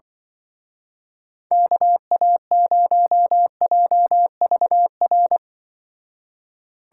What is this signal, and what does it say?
Morse code, heard as a keyed tone, about 25 wpm, 700 Hz. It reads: KA0JVR